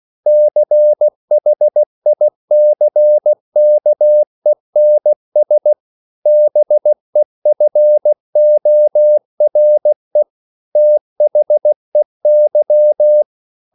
Morse code, heard as a keyed tone, about 16 wpm, 600 Hz. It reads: CHICKENS BEFORE THEY
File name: CW/Practice